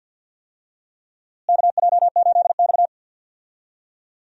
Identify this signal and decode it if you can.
Morse code, heard as a keyed tone, about 52 wpm, 700 Hz. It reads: KJ8X
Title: USA Call Signs - 4 Characters - Repeated 3x - 50wpm